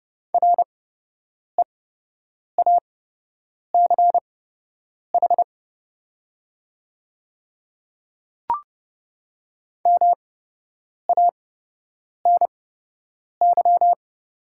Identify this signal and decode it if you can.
Morse code, heard as a keyed tone, about 30 wpm, spaced as for 8 wpm, 700 Hz. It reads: REACH MANY
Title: 8x Character Spacing - Top 500 Words - 30wpm